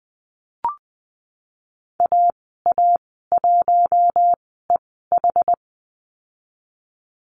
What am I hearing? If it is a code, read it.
Morse code, heard as a keyed tone, about 20 wpm, spaced as for 14 wpm, 700 Hz. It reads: AA1EH
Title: 069 Lesson 12g - Letter d - Call Signs 20wpm